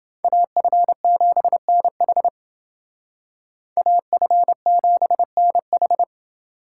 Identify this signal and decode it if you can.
Morse code, heard as a keyed tone, about 29 wpm, 700 Hz. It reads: AF7NH AF7NH